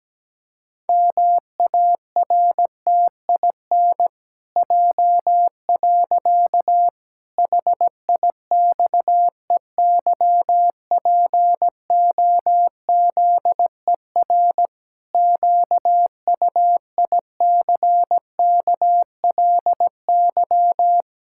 Morse code, heard as a keyed tone, about 17 wpm, 700 Hz. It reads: MARTIN J. HIXEYPOZER QUICKLY
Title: Pangram Challenge v2 - 17wpm